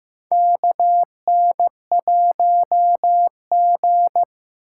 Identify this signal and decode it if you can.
Morse code, heard as a keyed tone, about 15 wpm, 700 Hz. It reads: KN1G
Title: USA Call Signs - 4 Characters - 2x Character Spacing - 15wpm